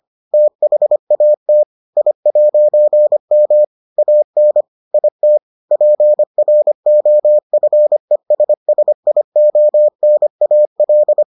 Morse code, heard as a keyed tone, about 25 wpm, 600 Hz. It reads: THAT I'M AN IT PROFESSIONAL